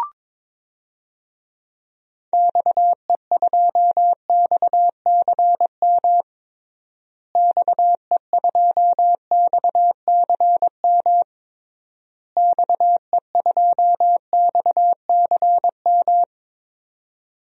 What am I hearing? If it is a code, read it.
Morse code, heard as a keyed tone, about 21 wpm, 700 Hz. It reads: XE2XCM XE2XCM XE2XCM
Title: DXCC-100 Call Signs - 6 Characters - Repeated 3x - 22wpm